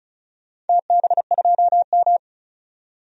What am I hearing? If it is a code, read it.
Morse code, heard as a keyed tone, about 36 wpm, 700 Hz. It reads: TB2M
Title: DXCC-100 Call Signs - 4 Characters - 35wpm